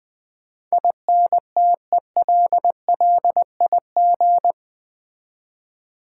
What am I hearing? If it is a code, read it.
Morse code, heard as a keyed tone, about 20 wpm, 700 Hz. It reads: INTELLIG
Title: Build A Word - 4 Letters At A Time - 12-Letter Words - 20wpm